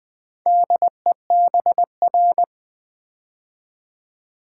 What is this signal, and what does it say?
Morse code, heard as a keyed tone, about 20 wpm, 700 Hz. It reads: DEBR
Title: Build A Word - 2 Letters At A Time - 6-Letter Words - 20wpm